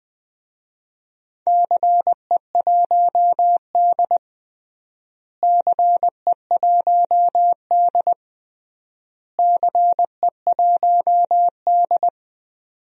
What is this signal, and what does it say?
Morse code, heard as a keyed tone, about 20 wpm, 700 Hz. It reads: CE1D CE1D CE1D
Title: DXCC-100 Call Signs - 4 Characters - Repeated 3x - 20wpm